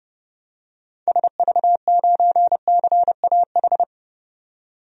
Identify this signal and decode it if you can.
Morse code, heard as a keyed tone, about 30 wpm, 700 Hz. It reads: SV9CAH
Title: DXCC-100 Call Signs - 6 Characters - 30wpm